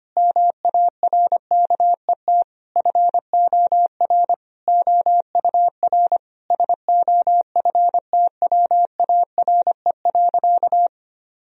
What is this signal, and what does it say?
Morse code, heard as a keyed tone, about 25 wpm, 700 Hz. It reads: MARKET FOR OUR SOFTWARE.
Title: Sentences from Top 1000 Words - Review Part 3 - 25wpm